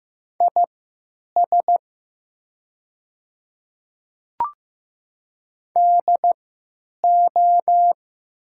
Morse code, heard as a keyed tone, about 15 wpm, spaced as for 9 wpm, 700 Hz. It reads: IS DO
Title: 3x Character Spacing - Top 200 Words - 15wpm